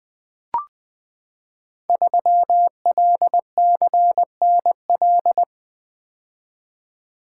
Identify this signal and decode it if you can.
Morse code, heard as a keyed tone, about 20 wpm, 700 Hz. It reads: 3LCNL
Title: Code Groups - 5 Letter:Numbers - 20wpm